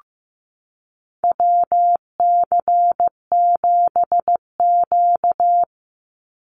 Morse code, heard as a keyed tone, about 15 wpm, 700 Hz. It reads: WC7Q
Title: Call Signs Repeated 2x - 15wpm